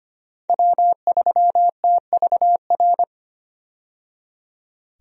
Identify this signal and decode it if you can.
Morse code, heard as a keyed tone, about 25 wpm, 700 Hz. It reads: W3TVR
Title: USA Call Signs - 5 Characters - 25wpm